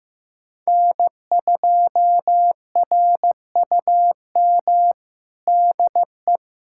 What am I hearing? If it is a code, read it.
Morse code, heard as a keyed tone, about 15 wpm, 700 Hz. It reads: N2RUM DE
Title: USA Callsign DE Callsign - 5 Characters - 15wpm